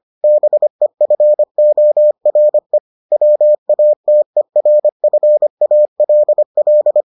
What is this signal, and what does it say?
Morse code, heard as a keyed tone, about 25 wpm, 600 Hz. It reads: BEFORE WATERFALL